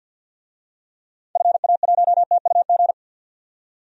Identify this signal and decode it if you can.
Morse code, heard as a keyed tone, about 48 wpm, 700 Hz. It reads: FA1TUG
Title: DXCC-100 Call Signs - 6 Characters - 50wpm